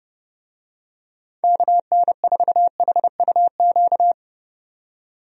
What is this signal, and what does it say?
Morse code, heard as a keyed tone, about 30 wpm, 700 Hz. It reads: KN4HUQ
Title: USA Call Signs - 6 Characters - 30wpm